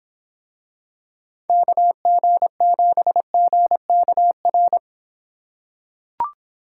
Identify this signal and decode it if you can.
Morse code, heard as a keyed tone, about 25 wpm, 700 Hz. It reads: KG7GKR